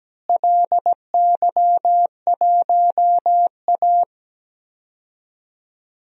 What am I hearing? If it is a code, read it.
Morse code, heard as a keyed tone, about 17 wpm, 700 Hz. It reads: LY1A
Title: DXCC-100 Call Signs - 4 Characters - Repeated 3x - 17wpm